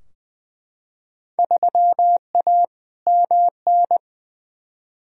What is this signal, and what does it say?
Morse code, heard as a keyed tone, about 21 wpm, 700 Hz. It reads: 3A MN
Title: Field Day Exchanges - 20wpm